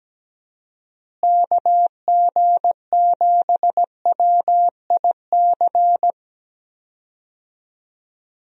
Morse code, heard as a keyed tone, about 17 wpm, 700 Hz. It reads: KG7WIC
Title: USA Call Signs - 6 Characters - Repeated 2x - 17wpm